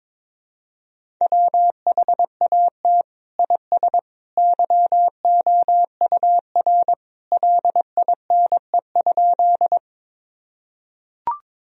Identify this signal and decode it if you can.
Morse code, heard as a keyed tone, about 22 wpm, 700 Hz. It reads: WHAT IS YOUR LINE?
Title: Sentences from Top 1000 Words - Review Part 1 - 22wpm